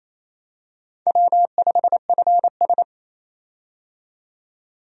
Morse code, heard as a keyed tone, about 28 wpm, 690 Hz. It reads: W5FS
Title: USA Call Signs - 4 Characters - 28wpm